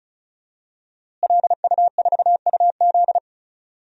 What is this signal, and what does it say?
Morse code, heard as a keyed tone, about 36 wpm, 700 Hz. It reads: LU4UZ
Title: DXCC-100 Call Signs - 5 Characters - Repeated 2x - 35wpm